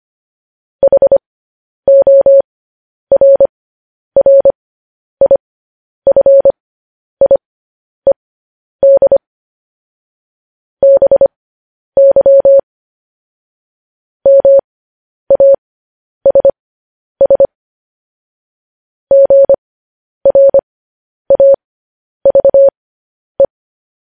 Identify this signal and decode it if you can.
Morse code, heard as a keyed tone, about 25 wpm, spaced as for 10 wpm, 550 Hz. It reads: HORRIFIED BY MASS GRAVE
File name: podcast / News headlines-1x-25x10